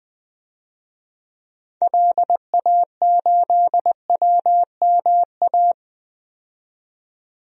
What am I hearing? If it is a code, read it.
Morse code, heard as a keyed tone, about 20 wpm, 700 Hz. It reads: LA8WMA
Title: DXCC-100 Call Signs - 6 Characters - 20wpm